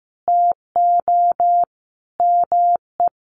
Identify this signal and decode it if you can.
Morse code, heard as a keyed tone, about 15 wpm, 700 Hz. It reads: TO ME
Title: Sentences from Top 1200 Words 15wpm